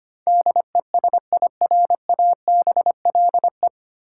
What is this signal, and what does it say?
Morse code, heard as a keyed tone, about 25 wpm, 700 Hz. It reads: DESIRABLE
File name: Nine Letter Words - 25wpm